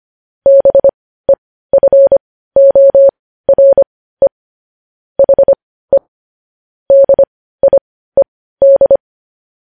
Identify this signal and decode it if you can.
Morse code, heard as a keyed tone, about 25 wpm, spaced as for 15 wpm, 550 Hz. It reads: BEFORE HE DIED
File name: podcast / News headlines-1x-25x15